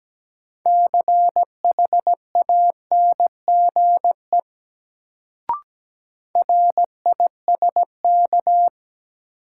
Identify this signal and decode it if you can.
Morse code, heard as a keyed tone, about 17 wpm, 700 Hz. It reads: CHANGE RISK
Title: Sentences from Top 600 Words - New Word Drill - 17wpm